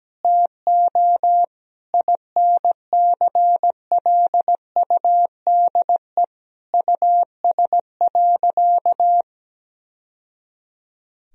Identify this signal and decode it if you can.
Morse code, heard as a keyed tone, about 17 wpm, 700 Hz. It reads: TO INCLUDE US.
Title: Sentences from Top 1000 Words - Review Part 3 - 17wpm